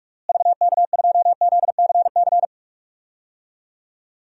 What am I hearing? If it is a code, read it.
Morse code, heard as a keyed tone, about 41 wpm, 700 Hz. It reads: VK2ZCC
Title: DXCC-100 Call Signs - 6 Characters - Repeated 2x - 45wpm